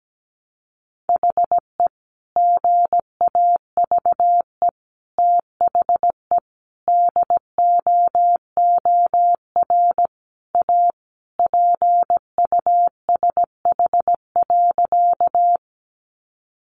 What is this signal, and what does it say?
Morse code, heard as a keyed tone, about 17 wpm, 700 Hz. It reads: HE GAVE THE DOOR A PUSH.
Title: Sentences from Top 1300 Words 17wpm